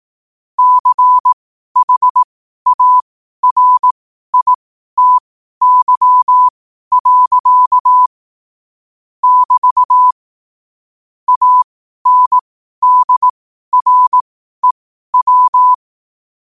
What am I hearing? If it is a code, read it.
Morse code, heard as a keyed tone, about 18 wpm, spaced as for 12 wpm, 1000 Hz. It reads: CHARITY. <BT> ANDREW
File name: qotdfiles / 12wpm